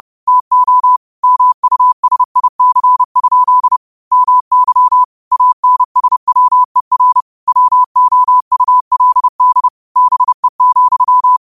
Morse code, heard as a keyed tone, about 30 wpm, 1000 Hz. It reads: TO MUSIC? MY ANSWER WOULD BE,
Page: New quotes every day in morse code at 30 Words per minute.